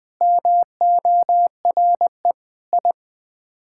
Morse code, heard as a keyed tone, about 20 wpm, 700 Hz. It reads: MORE I
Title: Sets of 9 Words - 20wpm